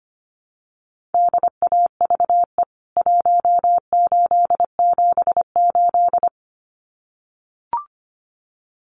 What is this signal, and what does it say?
Morse code, heard as a keyed tone, about 25 wpm, 700 Hz. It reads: DAVE 1878
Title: CWT Exchange 25wpm